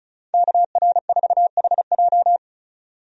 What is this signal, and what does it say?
Morse code, heard as a keyed tone, about 34 wpm, 700 Hz. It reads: KR4HJ
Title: USA Call Signs - 5 Characters - 35wpm